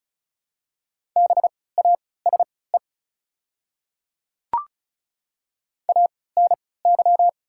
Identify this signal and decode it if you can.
Morse code, heard as a keyed tone, about 36 wpm, spaced as for 20 wpm, 700 Hz. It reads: BASE ANY
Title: 3x Character Spacing - Top 300 Words - 35wpm